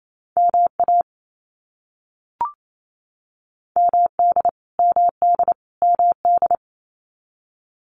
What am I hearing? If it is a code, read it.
Morse code, heard as a keyed tone, about 28 wpm, 700 Hz. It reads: MA MD MD MD
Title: US States Repeated 3x 28wpm